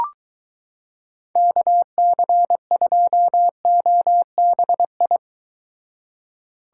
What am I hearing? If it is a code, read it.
Morse code, heard as a keyed tone, about 23 wpm, 700 Hz. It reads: KC2OBI